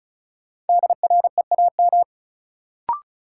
Morse code, heard as a keyed tone, about 36 wpm, 700 Hz. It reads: DREAM
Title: Sentences from Top 1500 Words - New Word Drill - 35wpm